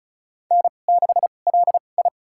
Morse code, heard as a keyed tone, about 36 wpm, spaced as for 25 wpm, 700 Hz. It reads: N6LI
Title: USA Call Signs - 4 Characters - 2x Character Spacing - 35wpm